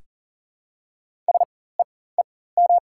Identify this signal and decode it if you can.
Morse code, heard as a keyed tone, about 39 wpm, spaced as for 19 wpm, 690 Hz. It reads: SEEM
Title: 4x Character Spacing - Top 300 Words - 40wpm